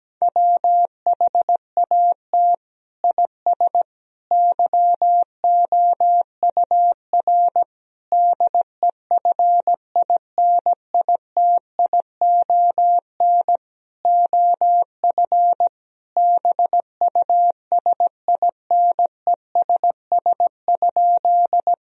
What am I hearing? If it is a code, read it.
Morse code, heard as a keyed tone, about 17 wpm, 700 Hz. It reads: WHAT IS YOUR DEFINITION OF BUSINESS?
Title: Sentences from Top 1000 Words - Review Part 4 - 17wpm